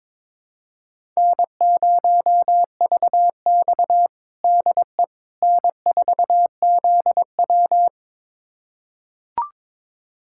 Morse code, heard as a keyed tone, about 22 wpm, 700 Hz. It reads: N0VX DE N4ZW
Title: USA Callsign DE Callsign - 4 Characters - 22wpm